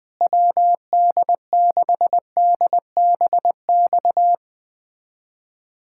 Morse code, heard as a keyed tone, about 20 wpm, 700 Hz. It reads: WD6DBX
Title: USA Callsign DE Callsign - 6 Characters - 20wpm